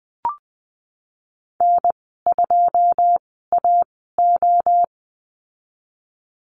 Morse code, heard as a keyed tone, about 20 wpm, spaced as for 15 wpm, 700 Hz. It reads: N2AO
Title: 122 Lesson 20g - Number 3 - Call Signs 20wpm